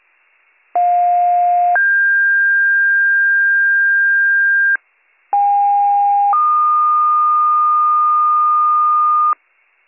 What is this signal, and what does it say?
Two-tone page, heard as a one-second tone, then a three-second tone, about 710 then 1600 Hz, about 810 then 1220 Hz, with noise. It is heard from 0.7 to 9.3 s.